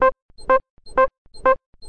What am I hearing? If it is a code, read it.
There is no code.